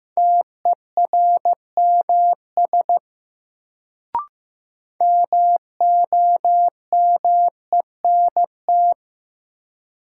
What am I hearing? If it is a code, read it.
Morse code, heard as a keyed tone, about 15 wpm, 700 Hz. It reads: TERMS MOMENT
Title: Sets of 3 Words - New Word Drill - 15wpm